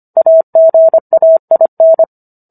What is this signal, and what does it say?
Morse code, heard as a keyed tone, about 25 wpm, 650 Hz. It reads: AGAIN